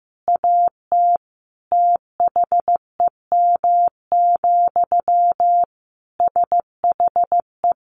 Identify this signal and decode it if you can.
Morse code, heard as a keyed tone, about 15 wpm, 700 Hz. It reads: AT THEM, SHE
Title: Sentences from Top 1400 Words 15wpm